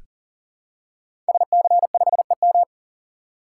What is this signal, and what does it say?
Morse code, heard as a keyed tone, about 41 wpm, 700 Hz. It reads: SC5EM
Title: Code Groups - 5 Letter:Numbers - 40wpm